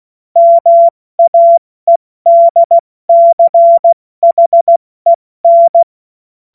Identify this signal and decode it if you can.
Morse code, heard as a keyed tone, about 16 wpm, 680 Hz. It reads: MAEDCHEN
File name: Morsekurs / Witze / Witz 08. November 2022 70 F80